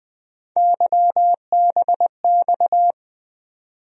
Morse code, heard as a keyed tone, about 20 wpm, 700 Hz. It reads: YBX
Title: USA Call Sign Postfixes - 3 Letters 20wpm